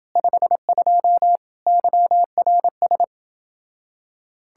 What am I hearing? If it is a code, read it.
Morse code, heard as a keyed tone, about 29 wpm, 700 Hz. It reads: 52 YRS